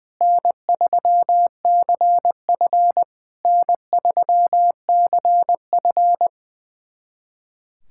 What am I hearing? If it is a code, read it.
Morse code, heard as a keyed tone, about 20 wpm, 700 Hz. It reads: N3CF N3CF
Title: SOTA QSO Practice 20wpm